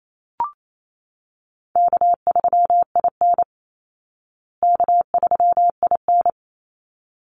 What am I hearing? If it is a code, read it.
Morse code, heard as a keyed tone, about 28 wpm, 700 Hz. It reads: K3IN K3IN
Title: Call Signs Repeated 3x - 28wpm